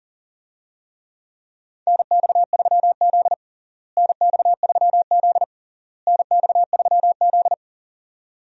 Morse code, heard as a keyed tone, about 40 wpm, 700 Hz. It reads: NX3Z NX3Z NX3Z
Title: USA Call Signs - 4 Characters - Repeated 3x - 40wpm